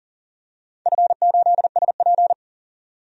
Morse code, heard as a keyed tone, about 44 wpm, 690 Hz. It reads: F8SP
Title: DXCC-100 Call Signs - 4 Characters - 40wpm